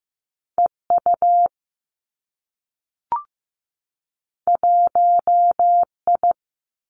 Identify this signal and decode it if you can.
Morse code, heard as a keyed tone, about 15 wpm, 700 Hz. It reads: E U 1 I
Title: Two Challenging Letter-Number Combinations 15wpm